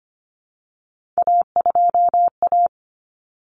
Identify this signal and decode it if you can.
Morse code, heard as a keyed tone, about 25 wpm, 700 Hz. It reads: A2A
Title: Common Cut Numbers - 3-Digit 25wpm